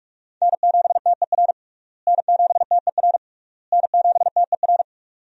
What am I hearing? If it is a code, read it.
Morse code, heard as a keyed tone, about 46 wpm, 700 Hz. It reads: N7TER N7TER N7TER
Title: USA Call Signs - 5 Characters - Repeated 3x - 45wpm